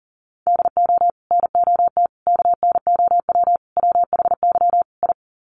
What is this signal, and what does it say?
Morse code, heard as a keyed tone, about 40 wpm, 700 Hz. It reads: DO NOT KNOW WHY I
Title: Sets of 5 Words Encore 40wpm